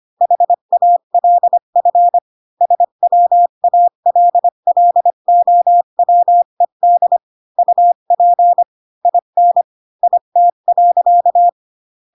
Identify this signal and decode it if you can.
Morse code, heard as a keyed tone, about 25 wpm, 700 Hz. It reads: HALF SWALLOWED UP IN IT.
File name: Pinocchio / Pinocchio ch3 20 25 ik1hge